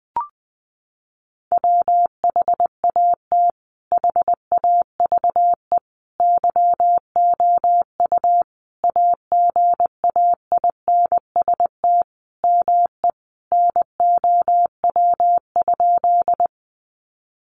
Morse code, heard as a keyed tone, about 20 wpm, 700 Hz. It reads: WHAT HAVE YOU AGAINST ME NOW?
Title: Sentences from Top 400 Words 20wpm